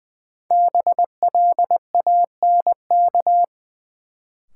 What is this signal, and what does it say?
Morse code, heard as a keyed tone, about 20 wpm, 700 Hz. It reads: BLANK
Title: Sentences from Top 1500 Words - New Word Drill - 20wpm